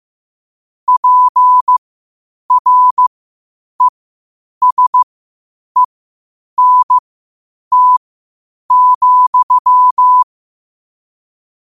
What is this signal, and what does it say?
Morse code, heard as a keyed tone, about 15 wpm, spaced as for 8 wpm, 1000 Hz. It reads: PRESENT,
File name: qotdfiles / 8wpm